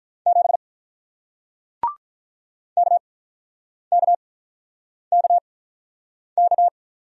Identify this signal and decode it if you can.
Morse code, heard as a keyed tone, about 48 wpm, spaced as for 9 wpm, 700 Hz. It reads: 7 KKKK